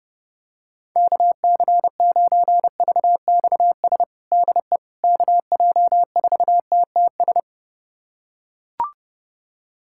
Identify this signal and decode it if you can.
Morse code, heard as a keyed tone, about 30 wpm, 700 Hz. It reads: KC9VXS DE KJ4TTS
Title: USA Callsign DE Callsign - 6 Characters - 30wpm